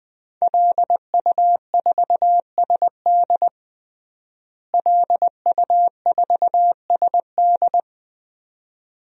Morse code, heard as a keyed tone, about 20 wpm, 700 Hz. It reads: LU4SD LU4SD
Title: DXCC-100 Call Signs - 5 Characters - Repeated 2x - 20wpm